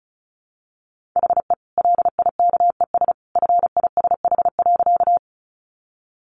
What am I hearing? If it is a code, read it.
Morse code, heard as a keyed tone, about 34 wpm, 700 Hz. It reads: HE LIKES FISH.
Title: Easy Phrases 35wpm